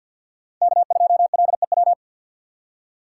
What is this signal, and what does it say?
Morse code, heard as a keyed tone, about 63 wpm, 700 Hz. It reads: K2LEW